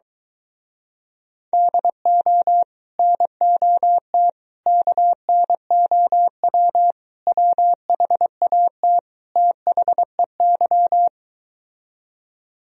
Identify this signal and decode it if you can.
Morse code, heard as a keyed tone, about 23 wpm, 700 Hz. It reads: DO NOT KNOW WHAT THEY